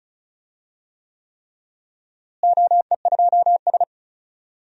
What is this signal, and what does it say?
Morse code, heard as a keyed tone, about 36 wpm, 700 Hz. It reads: OE2S
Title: DXCC-100 Call Signs - 4 Characters - Repeated 3x - 35wpm